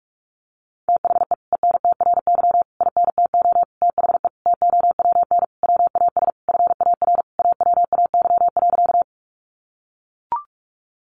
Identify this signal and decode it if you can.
Morse code, heard as a keyed tone, about 44 wpm, 700 Hz. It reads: THE ENTRY INTO THE TOWN WAS FAR AWAY.
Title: Sentences from Top 1500 Words 45wpm